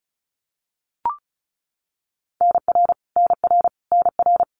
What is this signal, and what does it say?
Morse code, heard as a keyed tone, about 34 wpm, 700 Hz. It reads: NR NR NR
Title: Abbreviations Repeated 3x - 35wpm